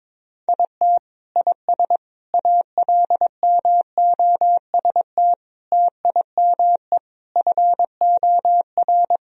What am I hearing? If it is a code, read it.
Morse code, heard as a keyed tone, about 22 wpm, 700 Hz. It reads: IT IS ALMOST TIME FOR
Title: Sentences from Top 1000 Words - Review Part 4 - 22wpm